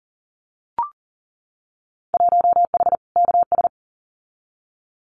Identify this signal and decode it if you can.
Morse code, heard as a keyed tone, about 40 wpm, 700 Hz. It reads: 1H KS
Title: Winter Field Day Exchanges 40wpm